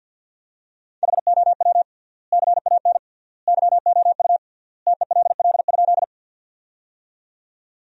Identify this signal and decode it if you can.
Morse code, heard as a keyed tone, about 52 wpm, 700 Hz. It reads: HOW CAN YOU TELL?
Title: 3x Word-Spacing - Sentences from Top 200 Words - 50wpm